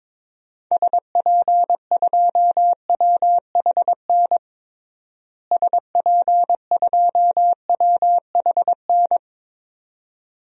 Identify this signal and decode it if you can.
Morse code, heard as a keyed tone, about 22 wpm, 700 Hz. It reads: SP2WHN SP2WHN
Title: DXCC-100 Call Signs - 6 Characters - Repeated 2x - 22wpm